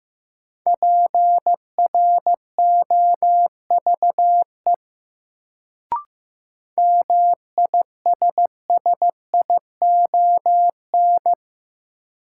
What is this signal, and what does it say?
Morse code, heard as a keyed tone, about 15 wpm, 700 Hz. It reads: PROVE MISSION
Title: Sentences from Top 1600 Words - New Word Drill Alt - 15wpm